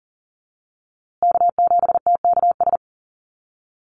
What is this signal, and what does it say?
Morse code, heard as a keyed tone, about 40 wpm, 700 Hz. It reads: K7TKS
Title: USA Call Signs Repeated 3x 40wpm